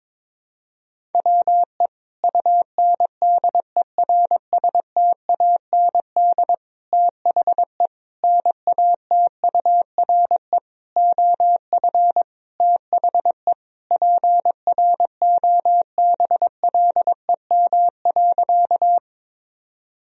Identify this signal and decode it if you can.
Morse code, heard as a keyed tone, about 21 wpm, 700 Hz. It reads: WE UNDERSTAND THE NATURE OF THE PROBLEM.
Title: Sentences from Top 1000 Words - Review Part 2 - 22wpm